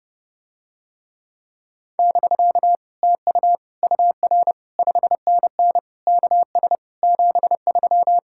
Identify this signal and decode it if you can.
Morse code, heard as a keyed tone, about 30 wpm, 700 Hz. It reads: <BK> TU UR 5NN KS 73